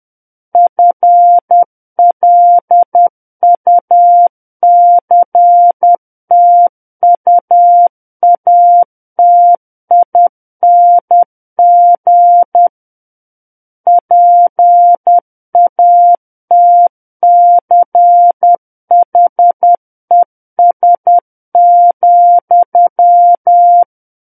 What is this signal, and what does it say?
Morse code, heard as a keyed tone, about 10 wpm, 700 Hz. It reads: FLUCTUATING PATCHES,
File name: Book One / 01-Chapter 1 - 10 WPM